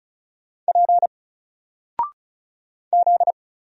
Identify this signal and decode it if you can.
Morse code, heard as a keyed tone, about 36 wpm, 700 Hz. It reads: P Z
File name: Single-Letter-Number-35wpm